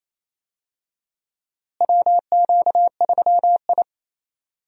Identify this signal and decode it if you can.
Morse code, heard as a keyed tone, about 28 wpm, 700 Hz. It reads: WQ3I
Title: USA Call Signs - 4 Characters - Repeated 2x - 28wpm